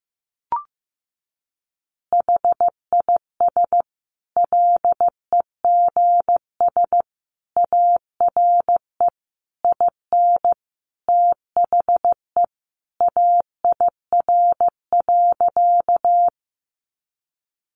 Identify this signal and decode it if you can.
Morse code, heard as a keyed tone, about 15 wpm, 700 Hz. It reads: HIS LEGS ARE IN THE AIR.
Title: Additional Easy Phrases 15wpm